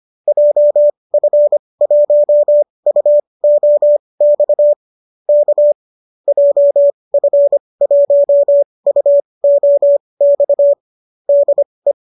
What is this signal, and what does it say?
Morse code, heard as a keyed tone, about 25 wpm, spaced as for 20 wpm, 590 Hz. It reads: JF1UOX K JF1UOX DE